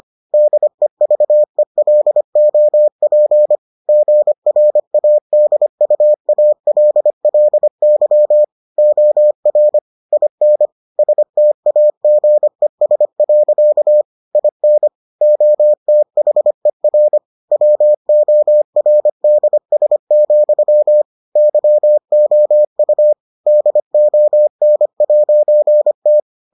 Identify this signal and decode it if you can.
Morse code, heard as a keyed tone, about 25 wpm, 600 Hz. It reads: DEVELOP GRADUALLY OR IN STAGES. IN OTHER WORDS, YOU DON'T